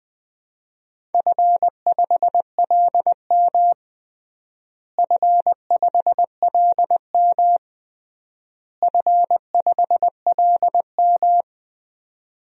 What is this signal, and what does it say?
Morse code, heard as a keyed tone, about 21 wpm, 700 Hz. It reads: F5LM F5LM F5LM
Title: DXCC-100 Call Signs - 4 Characters - Repeated 3x - 20wpm